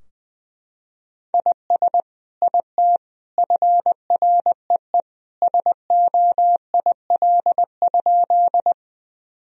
Morse code, heard as a keyed tone, about 20 wpm, 700 Hz. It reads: IS IT FREE SOIL?
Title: Sentences from Top 1000 Words - Review Part 4 - 20wpm